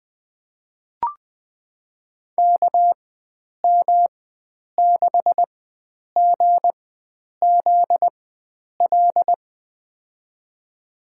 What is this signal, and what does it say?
Morse code, heard as a keyed tone, about 20 wpm, spaced as for 9 wpm, 700 Hz. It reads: KM6GZL
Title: USA Call Signs - 6 Characters - 4x Character Spacing - 20wpm